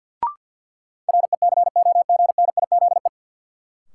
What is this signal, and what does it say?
Morse code, heard as a keyed tone, about 53 wpm, 700 Hz. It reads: RECOGNIZE
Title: Sentences from Top 1400 Words - New Word Drill - 50wpm